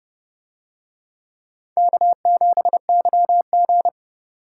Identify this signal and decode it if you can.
Morse code, heard as a keyed tone, about 28 wpm, 700 Hz. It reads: K7YG